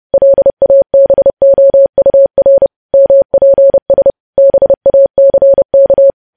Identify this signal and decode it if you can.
Morse code, heard as a keyed tone, about 30 wpm, 550 Hz. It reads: LABOUR MPS BACK
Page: news headlines in morse code at 30 wpm